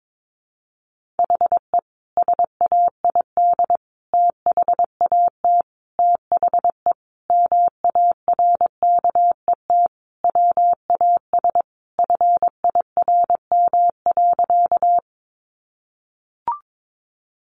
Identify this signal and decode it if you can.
Morse code, heard as a keyed tone, about 22 wpm, 700 Hz. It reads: HE SAID THAT THE MARKET WAS FIRM.
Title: Sentences from Top 1400 Words 22wpm